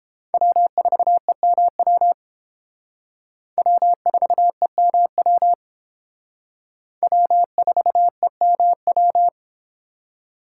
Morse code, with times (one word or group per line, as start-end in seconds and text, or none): 0.3-2.1 W4EMW
3.6-5.5 W4EMW
7.0-9.3 W4EMW